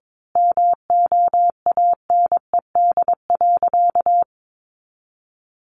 Morse code, heard as a keyed tone, about 22 wpm, 700 Hz. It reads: MOANED.
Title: A Sound Of Thunder 22wpm